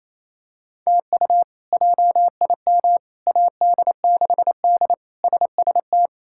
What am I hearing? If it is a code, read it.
Morse code, heard as a keyed tone, about 28 wpm, 700 Hz. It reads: TU JIM AD6D SST
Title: K1USN SST QSO Practice - 28wpm